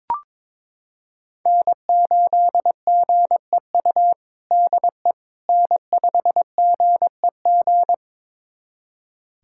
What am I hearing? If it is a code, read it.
Morse code, heard as a keyed tone, about 22 wpm, 700 Hz. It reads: N8GEU DE N5GEG
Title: USA Callsign DE Callsign - 5 Characters - 22wpm